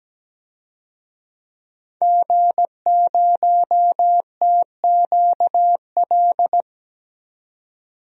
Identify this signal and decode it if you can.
Morse code, heard as a keyed tone, about 17 wpm, 700 Hz. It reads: G0TQL
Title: DXCC-100 Call Signs - 5 Characters - 17wpm